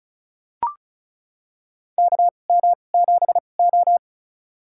Morse code, heard as a keyed tone, about 34 wpm, spaced as for 25 wpm, 700 Hz. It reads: KM7O
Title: USA Call Signs - 4 Characters - 2x Character Spacing - 35wpm